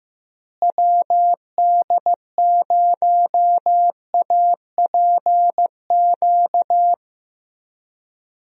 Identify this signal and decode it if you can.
Morse code, heard as a keyed tone, about 15 wpm, 700 Hz. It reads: WD0APQ
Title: USA Call Signs - 6 Characters - Repeated 3x - 15wpm